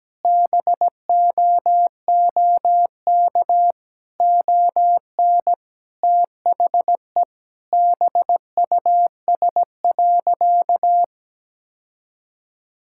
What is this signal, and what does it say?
Morse code, heard as a keyed tone, about 17 wpm, 700 Hz. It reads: BOOK ON THE BUS.
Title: Sentences from Top 1000 Words - Review Part 6 - 17wpm